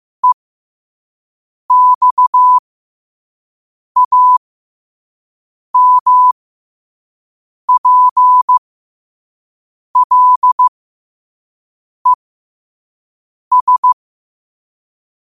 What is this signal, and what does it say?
Morse code, heard as a keyed tone, about 15 wpm, spaced as for 5 wpm, 1000 Hz. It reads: EXAMPLES
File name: qotdfiles / 5wpm